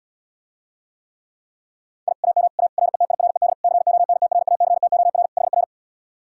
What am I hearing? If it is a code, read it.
Morse code, heard as a keyed tone, about 77 wpm, 700 Hz. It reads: I AM A RETIRED CONTRACTOR HR